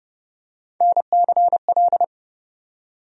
Morse code, heard as a keyed tone, about 32 wpm, 700 Hz. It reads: NCL
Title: USA Call Sign Postfixes - 3 Letters 30wpm